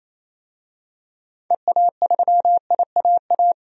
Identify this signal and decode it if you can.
Morse code, heard as a keyed tone, about 28 wpm, 700 Hz. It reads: EA3IAA
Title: DXCC-100 Call Signs - 6 Characters - 28wpm